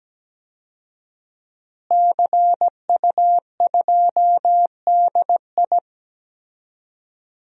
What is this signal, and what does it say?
Morse code, heard as a keyed tone, about 17 wpm, 700 Hz. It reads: CU2DI
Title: DXCC-100 Call Signs - 5 Characters - Repeated 2x - 17wpm